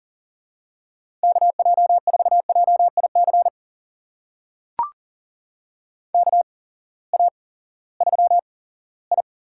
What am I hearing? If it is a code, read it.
Morse code, heard as a keyed tone, about 40 wpm, spaced as for 20 wpm, 700 Hz. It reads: KJ4JIC KA3I
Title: USA Call Signs - 6 Characters - 8x Character Spacing - 40wpm